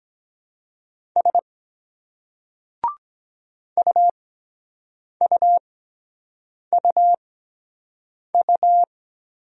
Morse code, heard as a keyed tone, about 23 wpm, spaced as for 7 wpm, 700 Hz. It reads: S UUUU